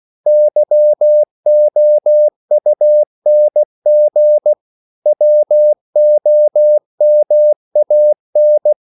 Morse code, heard as a keyed tone, about 16 wpm, 600 Hz. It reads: YOUNG WOMAN